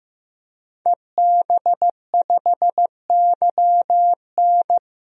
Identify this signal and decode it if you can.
Morse code, heard as a keyed tone, about 15 wpm, 700 Hz. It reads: EB5YN
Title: DXCC-100 Call Signs - 5 Characters - Repeated 2x - 15wpm